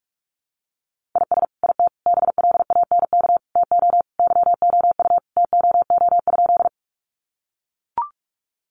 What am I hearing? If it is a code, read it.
Morse code, heard as a keyed tone, about 44 wpm, 700 Hz. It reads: IS IT BLANK TO YOU TOO?
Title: Sentences from Top 1500 Words 45wpm